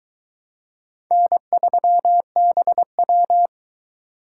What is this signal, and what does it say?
Morse code, heard as a keyed tone, about 22 wpm, 700 Hz. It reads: N3BW